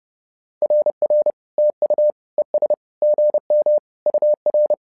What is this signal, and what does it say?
Morse code, heard as a keyed tone, about 30 wpm, 600 Hz. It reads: RR TU ES GM UR
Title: SOTA QSO Practice 30wpm